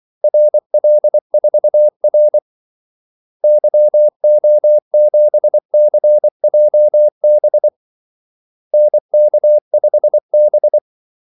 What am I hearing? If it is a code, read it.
Morse code, heard as a keyed tone, about 24 wpm, 600 Hz. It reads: RL4R YO7CJB NK5B
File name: Callsigns-24wpm